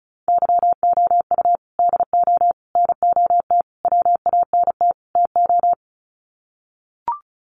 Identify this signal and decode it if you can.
Morse code, heard as a keyed tone, about 34 wpm, 700 Hz. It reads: YOU DO NOT WANT TO
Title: Sets of 5 Words 35wpm